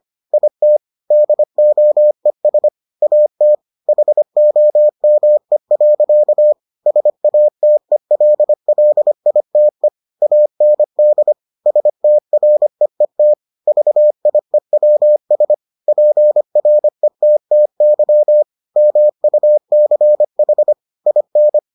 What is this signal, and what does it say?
Morse code, heard as a keyed tone, about 25 wpm, 600 Hz. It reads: IT DOES AT HOME. SATELLITE AND STREET VIEWS PRETTY MUCH IN